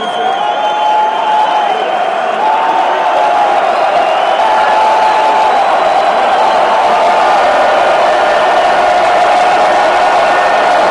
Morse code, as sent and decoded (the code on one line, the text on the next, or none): none